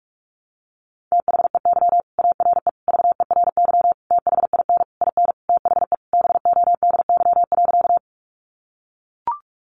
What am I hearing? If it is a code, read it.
Morse code, heard as a keyed tone, about 44 wpm, 700 Hz. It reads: THEY ARE VERY THIN IN THE BODY.
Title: Sentences from Top 1200 Words 45wpm